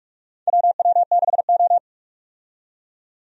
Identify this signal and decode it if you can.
Morse code, heard as a keyed tone, about 45 wpm, 700 Hz. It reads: WW6O
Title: USA Call Signs - 4 Characters - 45wpm